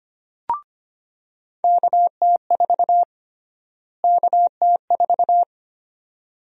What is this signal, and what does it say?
Morse code, heard as a keyed tone, about 25 wpm, 700 Hz. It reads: KT4 KT4
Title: USA Call Sign Prefixes - Repeated 2x - 25wpm